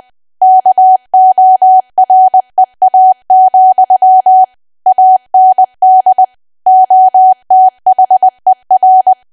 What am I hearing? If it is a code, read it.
Morse code, heard as a keyed tone, about 20 wpm, 750 Hz. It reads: KOREA, AND OTHER
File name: cwa / W1AW-Level3 / 20WPM